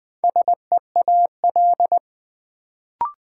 Morse code, heard as a keyed tone, about 20 wpm, 700 Hz. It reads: SEAL
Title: Build A Word - 2 Letters At A Time - 4-Letter Words - 20wpm